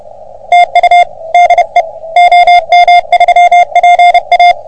Morse code, heard as a keyed tone, about 30 wpm, 720 Hz. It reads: TU DE OM3PA
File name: OM3PA PILUP CUC 388